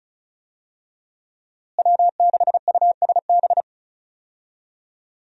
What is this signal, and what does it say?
Morse code, heard as a keyed tone, about 34 wpm, 700 Hz. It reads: W6USB
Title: USA Call Signs - 5 Characters - Repeated 2x - 35wpm